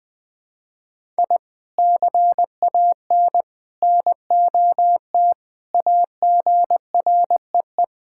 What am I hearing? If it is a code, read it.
Morse code, heard as a keyed tone, about 20 wpm, 700 Hz. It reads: I CAN NOT AGREE
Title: Sentences with 3-Word Phrases - Spoken Beginning - 20wpm